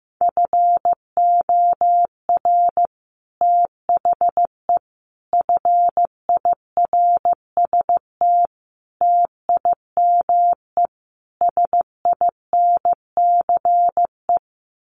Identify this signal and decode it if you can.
Morse code, heard as a keyed tone, about 14 wpm, 700 Hz. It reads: FOR THE FIRST TIME SINCE
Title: Sets of 5 Words 15wpm